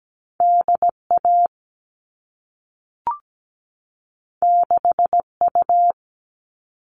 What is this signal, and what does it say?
Morse code, heard as a keyed tone, about 17 wpm, 700 Hz. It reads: DA 6U
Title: All Cut Numbers - 2-Digit 17wpm